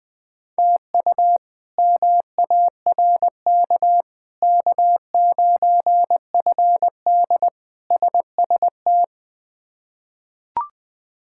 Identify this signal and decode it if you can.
Morse code, heard as a keyed tone, about 20 wpm, 700 Hz. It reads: TU MARK K9FD SST
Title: K1USN SST QSO Practice - 20wpm